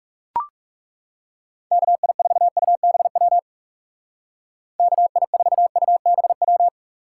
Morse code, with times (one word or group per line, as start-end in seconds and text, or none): 1.7-3.4 KI4UBW
4.8-6.7 KI4UBW